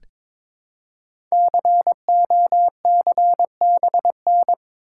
Morse code, heard as a keyed tone, about 22 wpm, 700 Hz. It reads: COCBN
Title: Code Groups - 5 Letters - 22wpm